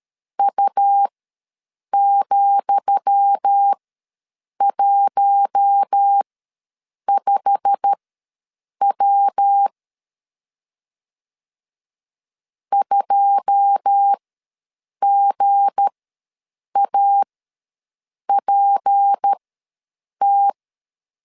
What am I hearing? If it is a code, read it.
Morse code, heard as a keyed tone, about 13 wpm, spaced as for 7 wpm, 800 Hz. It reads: U,15W 2GAPT
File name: LEZIONI / 10 SIGNS / 10D COMA ws 12wpm 3618